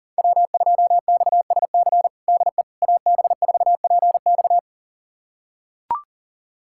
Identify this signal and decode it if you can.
Morse code, heard as a keyed tone, about 40 wpm, 700 Hz. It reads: W2XSC DE AB4PX
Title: USA Callsign DE Callsign - 5 Characters - 40wpm